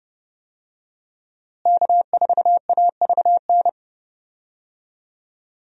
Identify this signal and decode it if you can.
Morse code, heard as a keyed tone, about 29 wpm, 700 Hz. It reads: K4AVN
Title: USA Call Signs - 5 Characters - Repeated 3x - 30wpm